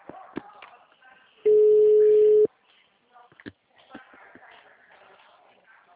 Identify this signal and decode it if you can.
Ringback tone (Europe, 425 Hz), with speech.